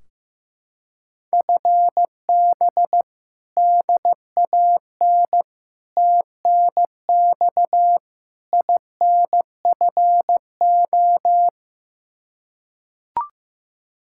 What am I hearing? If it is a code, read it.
Morse code, heard as a keyed tone, about 15 wpm, 700 Hz. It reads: FB DAN TNX INFO
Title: Practice QSO - Exchange 3 - More Info - 15wpm